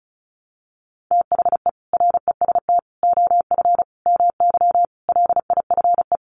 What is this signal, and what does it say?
Morse code, heard as a keyed tone, about 34 wpm, 700 Hz. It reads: THE REST OF MY LIFE
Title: Sets of 5 Words 35wpm